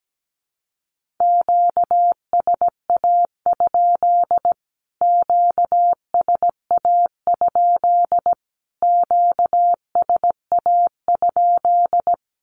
Morse code, heard as a keyed tone, about 17 wpm, 700 Hz. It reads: QSA? QSA? QSA?
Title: Q-Codes Repeated 3x 17wpm